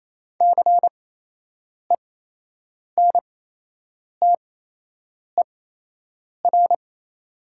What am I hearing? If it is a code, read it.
Morse code, heard as a keyed tone, about 28 wpm, spaced as for 8 wpm, 700 Hz. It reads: CENTER
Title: 8x Character Spacing - Top 500 Words - 28wpm